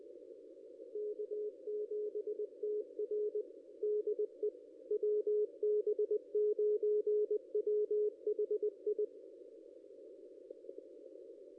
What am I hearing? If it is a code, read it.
Morse code, heard as a keyed tone, about 20 wpm, 420 Hz. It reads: K7TR DE WB9WHI
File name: contests / ss-cw-k5tr-op-n5rz -1900